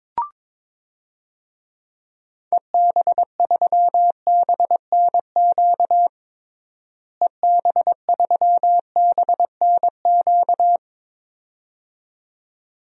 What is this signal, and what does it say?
Morse code, heard as a keyed tone, about 22 wpm, 700 Hz. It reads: EB3BNQ EB3BNQ
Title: DXCC-100 Call Signs - 6 Characters - Repeated 2x - 22wpm